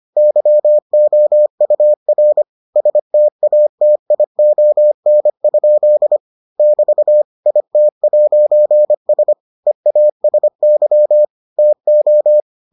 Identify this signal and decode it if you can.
Morse code, heard as a keyed tone, about 25 wpm, 600 Hz. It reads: YOUR STATION? <BT> IT'S EASY TO